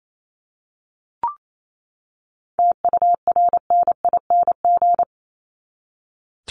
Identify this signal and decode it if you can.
Morse code, heard as a keyed tone, about 28 wpm, 700 Hz. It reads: TURNING
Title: ING Suffix - Top 100 Words with Base Words 28wpm